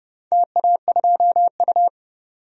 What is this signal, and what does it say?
Morse code, heard as a keyed tone, about 30 wpm, 700 Hz. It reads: TA2U
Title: DXCC-100 Call Signs - 4 Characters - 30wpm